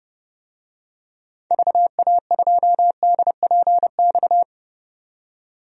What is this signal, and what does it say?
Morse code, heard as a keyed tone, about 30 wpm, 700 Hz. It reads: VA2DPX
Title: DXCC-100 Call Signs - 6 Characters - 30wpm